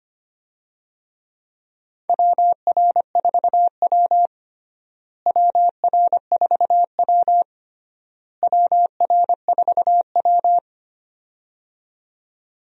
Morse code, heard as a keyed tone, about 25 wpm, 700 Hz. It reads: WR4W WR4W WR4W
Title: USA Call Signs - 4 Characters - Repeated 3x - 25wpm